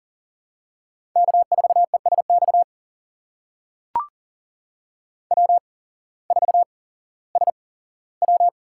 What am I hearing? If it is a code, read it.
Morse code, heard as a keyed tone, about 37 wpm, spaced as for 18 wpm, 700 Hz. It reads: K4ESX W4SW
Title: USA Call Signs - 5 Characters - 8x Character Spacing - 40wpm